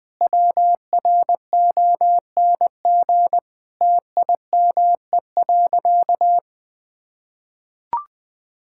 Morse code, heard as a keyed tone, about 20 wpm, 700 Hz. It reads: WRONG TIME.
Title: Sentences with 9-Word Phrases - 20wpm